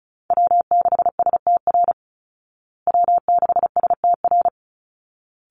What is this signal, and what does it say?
Morse code, heard as a keyed tone, about 34 wpm, 700 Hz. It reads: W6STR W6STR
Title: Call Signs Repeated 3x - 35wpm